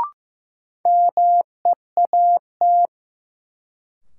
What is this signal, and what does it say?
Morse code, heard as a keyed tone, about 15 wpm, 700 Hz. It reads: MEAT
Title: Sentences from Top 1000 Words - Review Part 3 - New Word Drill - 15wpm